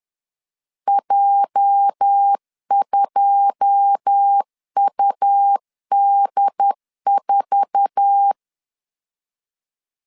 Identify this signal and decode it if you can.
Morse code, heard as a keyed tone, about 10 wpm, 800 Hz. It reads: J2UD4
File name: LEZIONI / 9 2867 10wpm 2733